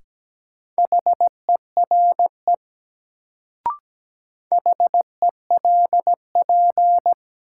Morse code, heard as a keyed tone, about 17 wpm, 700 Hz. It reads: HERE HELP
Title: Sentences from Top 200 Words - New Word Drill - 17wpm